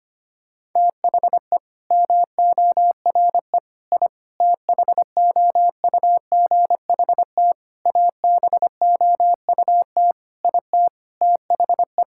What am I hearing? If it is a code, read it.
Morse code, heard as a keyed tone, about 25 wpm, 700 Hz. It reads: THE MORE I THOUGHT ABOUT IT THE
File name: practice / 25wpm / Sentences with 9-Word Phrases - 25wpm